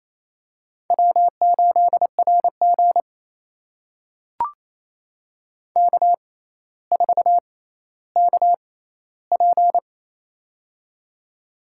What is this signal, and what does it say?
Morse code, heard as a keyed tone, about 28 wpm, spaced as for 14 wpm, 700 Hz. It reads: W8RG K4KP
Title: USA Call Signs - 4 Characters - 6x Character Spacing - 28wpm